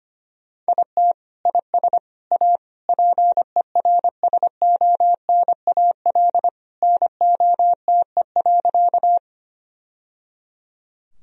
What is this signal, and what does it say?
Morse code, heard as a keyed tone, about 25 wpm, 700 Hz. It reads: IT IS A PERSONAL NOTE.
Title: Sentences from Top 1000 Words - Review Part 4 - 25wpm